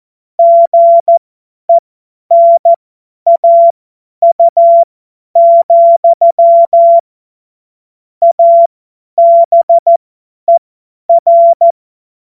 Morse code, heard as a keyed tone, about 14 wpm, spaced as for 10 wpm, 680 Hz. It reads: GENAU, ABER
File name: Morsekurs / Witze / Witz 22. August 2023 50 F70